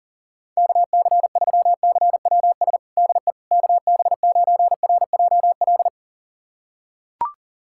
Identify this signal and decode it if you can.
Morse code, heard as a keyed tone, about 37 wpm, 700 Hz. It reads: KC3CWS DE KB9RJL
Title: USA Callsign DE Callsign - 6 Characters - 40wpm